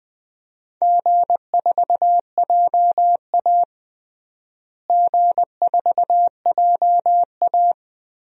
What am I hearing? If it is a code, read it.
Morse code, heard as a keyed tone, about 21 wpm, 700 Hz. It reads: G4JA G4JA
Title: DXCC-100 Call Signs - 4 Characters - Repeated 3x - 20wpm